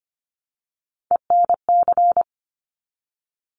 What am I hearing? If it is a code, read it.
Morse code, heard as a keyed tone, about 25 wpm, 700 Hz. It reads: ENC
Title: Top Three Letter Combinations in Words 25wpm